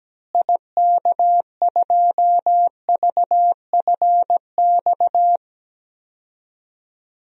Morse code, heard as a keyed tone, about 17 wpm, 700 Hz. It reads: IK2VFX
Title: DXCC-100 Call Signs - 6 Characters - Repeated 2x - 17wpm